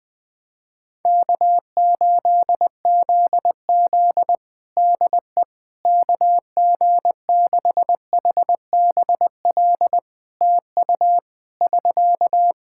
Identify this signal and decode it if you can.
Morse code, heard as a keyed tone, about 20 wpm, 700 Hz. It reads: K8ZZ DE KG6HBL TU <SK>
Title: Practice QSO - Exchange 4 - Ending - 20wpm